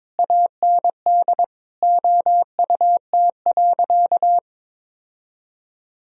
Morse code, heard as a keyed tone, about 22 wpm, 700 Hz. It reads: AND OUT.
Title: Sentences from Top 1000 Words - Review Part 2 - 22wpm